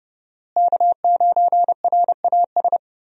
Morse code, heard as a keyed tone, about 31 wpm, 700 Hz. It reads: K9RAS
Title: USA Call Signs - 5 Characters - Repeated 2x - 30wpm